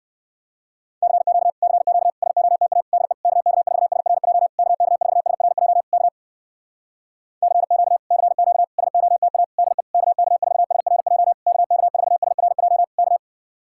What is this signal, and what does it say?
Morse code, heard as a keyed tone, about 59 wpm, 700 Hz. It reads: CQ CQ SOTA DE KK4SRJ KK4SRJ K CQ CQ SOTA DE KK4SRJ KK4SRJ K